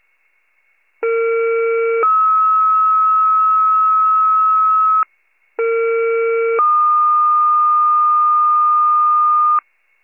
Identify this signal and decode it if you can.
Two-tone page, heard as a one-second tone, then a three-second tone, about 460 then 1290 Hz, about 460 then 1220 Hz. It is heard from 1.0 to 9.6 s.